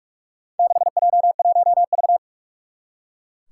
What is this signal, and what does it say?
Morse code, heard as a keyed tone, about 44 wpm, 700 Hz. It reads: 6J1V
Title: Four Challenging Letter-Number Combinations 45wpm